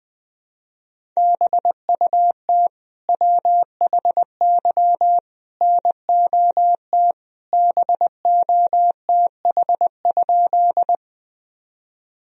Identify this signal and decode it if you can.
Morse code, heard as a keyed tone, about 20 wpm, 700 Hz. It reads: BUT WHY NOT BOTH?
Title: Sentences from Top 1000 Words - Review Part 1 - 20wpm